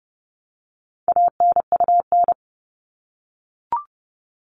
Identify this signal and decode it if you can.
Morse code, heard as a keyed tone, about 30 wpm, 700 Hz. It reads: ANUN
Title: All Cut Numbers - 4-Digit 30wpm